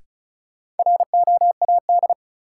Morse code, heard as a keyed tone, about 36 wpm, 700 Hz. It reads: ROAD
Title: Sentences from Top 400 Words - New Word Drill - 35wpm